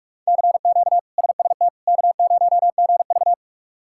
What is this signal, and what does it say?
Morse code, heard as a keyed tone, about 44 wpm, 700 Hz. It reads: CQ SST K0GV
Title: K1USN SST QSO Practice - 45wpm